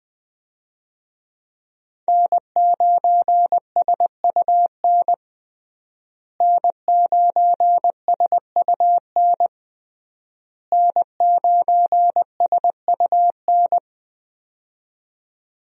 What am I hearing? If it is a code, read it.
Morse code, heard as a keyed tone, about 20 wpm, 700 Hz. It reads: N9SUN N9SUN N9SUN
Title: USA Call Signs - 5 Characters - Repeated 3x - 20wpm